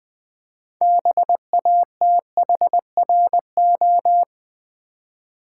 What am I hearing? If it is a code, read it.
Morse code, heard as a keyed tone, about 20 wpm, 700 Hz. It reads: BATHRO
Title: Build A Word - 3 Letters At A Time - 8-Letter Words - 20wpm